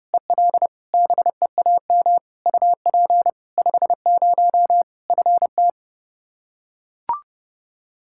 Morse code, heard as a keyed tone, about 32 wpm, 700 Hz. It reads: EL BEAM UP 50 FT
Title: K1USN SST QSO Practice 30wpm